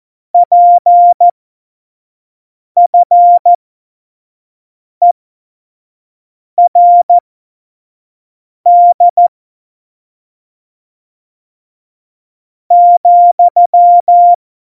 Morse code, heard as a keyed tone, about 14 wpm, spaced as for 5 wpm, 700 Hz. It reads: PFERD ,